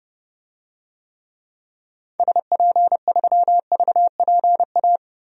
Morse code, heard as a keyed tone, about 30 wpm, 700 Hz. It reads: SP3VPA
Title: DXCC-100 Call Signs - 6 Characters - 30wpm